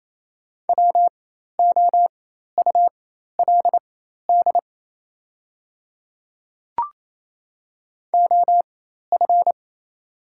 Morse code, heard as a keyed tone, about 28 wpm, spaced as for 13 wpm, 700 Hz. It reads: WOULD OF
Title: 4x Character Spacing - Top 100 Words - 28wpm